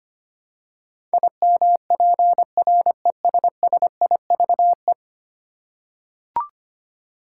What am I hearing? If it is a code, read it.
Morse code, heard as a keyed tone, about 25 wpm, 700 Hz. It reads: IMPRESSIVE
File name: 10 Letter Words - First 3 Letters Given - 25wpm